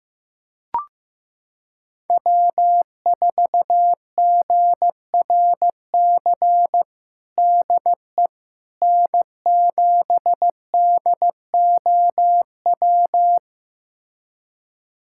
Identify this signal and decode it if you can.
Morse code, heard as a keyed tone, about 15 wpm, 700 Hz. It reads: W4GRC DE N7DOW
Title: USA Callsign DE Callsign - 5 Characters - 15wpm